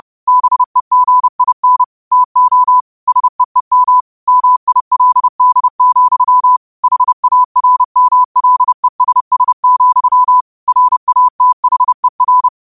Morse code, heard as a keyed tone, about 30 wpm, 1000 Hz. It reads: BEGIN TO SEEM MILD, HARMLESS, RATHER
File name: qotdfiles / 30wpm